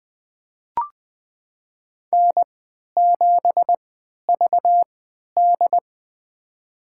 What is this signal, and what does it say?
Morse code, heard as a keyed tone, about 21 wpm, spaced as for 11 wpm, 700 Hz. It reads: N7VD
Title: USA Call Signs - 4 Characters - 3x Character Spacing - 20wpm